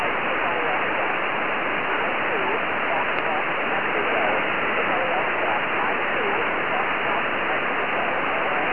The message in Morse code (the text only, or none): none